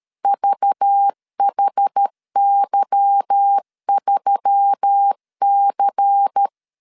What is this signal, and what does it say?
Morse code, heard as a keyed tone, about 13 wpm, 800 Hz. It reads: VHY3C
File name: CW / DET / 1 9 1500C S C 12wpm 3811